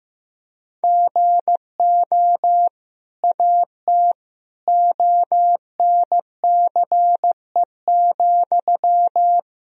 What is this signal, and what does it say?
Morse code, heard as a keyed tone, about 15 wpm, 700 Hz. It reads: GO AT ONCE,
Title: Sentences from Top 1000 Words - Review Part 1 - 15wpm